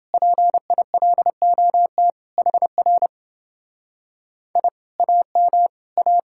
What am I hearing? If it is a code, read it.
Morse code, heard as a keyed tone, about 28 wpm, 700 Hz. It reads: PILOT HR I AM A